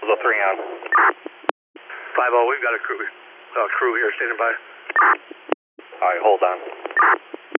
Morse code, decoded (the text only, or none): none